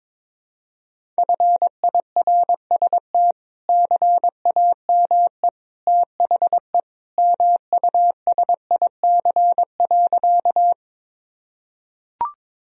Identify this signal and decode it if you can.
Morse code, heard as a keyed tone, about 22 wpm, 700 Hz. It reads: FIRST CAME THE MUSIC.
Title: Sentences from Top 1000 Words - Review Part 1 - 22wpm